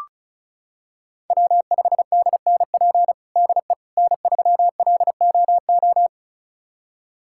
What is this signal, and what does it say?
Morse code, heard as a keyed tone, about 34 wpm, 700 Hz. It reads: W5DNP DE N3LOO
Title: USA Callsign DE Callsign - 5 Characters - 35wpm